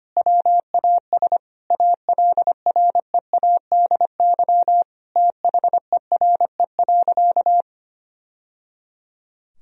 Morse code, heard as a keyed tone, about 25 wpm, 700 Hz. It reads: WAS ALREADY THERE.
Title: Sentences from Top 1000 Words - Review Part 1 - 25wpm